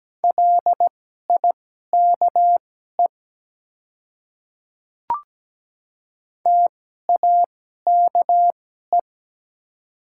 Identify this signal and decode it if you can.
Morse code, heard as a keyed tone, about 17 wpm, spaced as for 12 wpm, 700 Hz. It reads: LIKE TAKE
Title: 2x Character Spacing - Top 200 Words - 17wpm